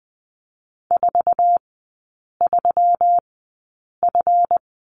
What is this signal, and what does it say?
Morse code, heard as a keyed tone, about 20 wpm, spaced as for 8 wpm, 700 Hz. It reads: 43F
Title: 198 Lesson 32e - Letter k - Three Characters 20wpm